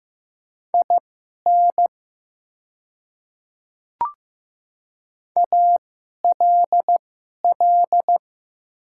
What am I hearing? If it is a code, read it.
Morse code, heard as a keyed tone, about 15 wpm, spaced as for 11 wpm, 700 Hz. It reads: IN ALL
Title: 2x Character Spacing - Top 100 Words - 15wpm